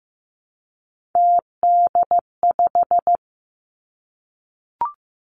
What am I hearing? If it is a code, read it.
Morse code, heard as a keyed tone, about 15 wpm, 700 Hz. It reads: TD5
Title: Three Challenging Letter-Number Combinations 15wpm